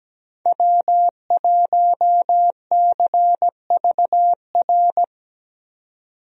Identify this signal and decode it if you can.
Morse code, heard as a keyed tone, about 17 wpm, 700 Hz. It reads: W1CVR
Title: USA Call Signs - 5 Characters - 17wpm